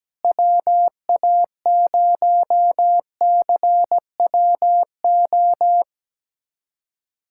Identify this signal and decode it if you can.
Morse code, heard as a keyed tone, about 17 wpm, 700 Hz. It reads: WA0CWO
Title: USA Callsign DE Callsign - 6 Characters - 17wpm